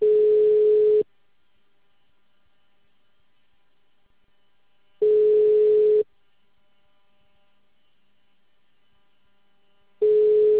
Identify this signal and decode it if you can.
Ringback tone (Europe, 425 Hz).
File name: out- -2044- - - .698